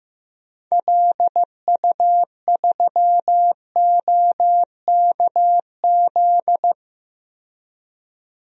Morse code, heard as a keyed tone, about 15 wpm, 700 Hz. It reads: LU3OKZ
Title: DXCC-100 Call Signs - 6 Characters - 15wpm